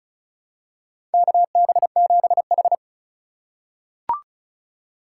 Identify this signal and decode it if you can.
Morse code, heard as a keyed tone, about 34 wpm, 700 Hz. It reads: KB7H
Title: K1USN SST QSO Practice - 35wpm